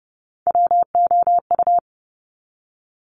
Morse code, heard as a keyed tone, about 30 wpm, 700 Hz. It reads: WOU
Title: Top Three Letter Combinations in Words 30wpm